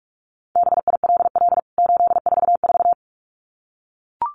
Morse code, heard as a keyed tone, about 44 wpm, 700 Hz. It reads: BILL 844
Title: CWT Exchange 45wpm